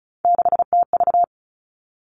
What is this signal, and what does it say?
Morse code, heard as a keyed tone, about 36 wpm, 700 Hz. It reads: 6TV
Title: Three Challenging Letter-Number Combinations 35wpm